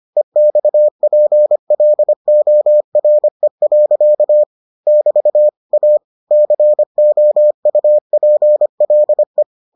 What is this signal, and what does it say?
Morse code, heard as a keyed tone, about 25 wpm, 600 Hz. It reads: EXPLORE. <BT> A COUPLE